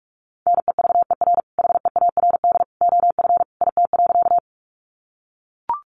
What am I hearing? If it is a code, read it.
Morse code, heard as a keyed tone, about 44 wpm, 700 Hz. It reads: NEVER HEARD OF IT.
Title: Sentences from Top 400 Words 45wpm